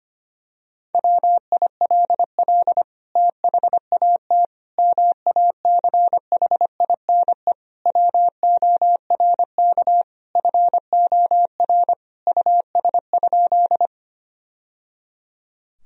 Morse code, heard as a keyed tone, about 25 wpm, 700 Hz. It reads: WILL THAT MACHINE WORK FOR US?
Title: Sentences from Top 1000 Words - Review Part 2 - 25wpm